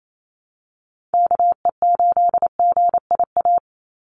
Morse code, heard as a keyed tone, about 28 wpm, 700 Hz. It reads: KE8GIA
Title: USA Call Signs 28wpm